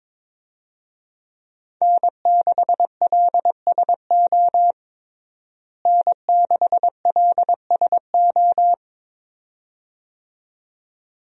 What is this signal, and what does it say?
Morse code, heard as a keyed tone, about 21 wpm, 700 Hz. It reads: N6LSO N6LSO
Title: USA Call Signs - 5 Characters - Repeated 2x - 22wpm